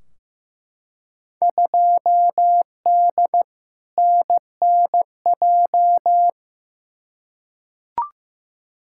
Morse code, heard as a keyed tone, about 15 wpm, 700 Hz. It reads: 2D NNJ
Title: Field Day Exchanges - 15wpm